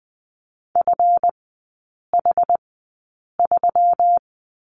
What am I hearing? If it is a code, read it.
Morse code, heard as a keyed tone, about 20 wpm, spaced as for 8 wpm, 690 Hz. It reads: FH3
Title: 159 Lesson 26e - Letter g - Three Characters 20wpm